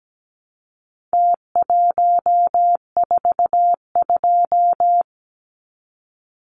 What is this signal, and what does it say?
Morse code, heard as a keyed tone, about 17 wpm, 700 Hz. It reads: T142
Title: Numbers - Contest Style - 4-Digit 17wpm